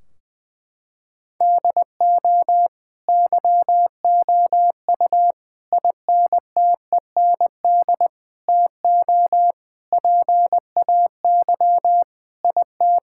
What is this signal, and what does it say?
Morse code, heard as a keyed tone, about 21 wpm, 700 Hz. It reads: DO YOU INTEND TO PAY IT
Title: Sentences from Top 1600 Words 20wpm